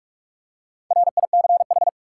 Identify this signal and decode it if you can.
Morse code, heard as a keyed tone, about 46 wpm, 700 Hz. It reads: RICH
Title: Four Letter Words - 45wpm